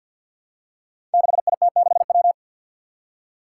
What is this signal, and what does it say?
Morse code, heard as a keyed tone, about 53 wpm, 700 Hz. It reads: 6IT6W
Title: Five Challenging Letter-Number Combinations 50wpm